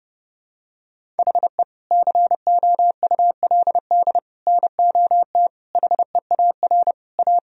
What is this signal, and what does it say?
Morse code, heard as a keyed tone, about 30 wpm, 700 Hz. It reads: HE COULD NOT HEAR A
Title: Sentences from Top 1000 Words - Review Part 1 - 30wpm